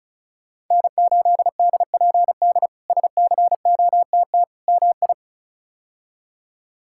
Morse code, heard as a keyed tone, about 36 wpm, 700 Hz. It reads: N8DPD SCOTT MI
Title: K1USN SST QSO Practice - 35wpm